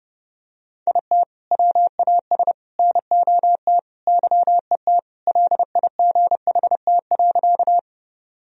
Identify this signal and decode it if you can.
Morse code, heard as a keyed tone, about 32 wpm, 700 Hz. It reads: IT WAS NOT YET LIGHT.
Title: Sentences from Top 1000 Words - Review Part 1 - 30wpm